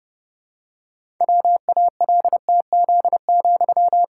Morse code, heard as a keyed tone, about 30 wpm, 700 Hz. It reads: WALTZ,
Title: Pangram Challenge v2 - 30wpm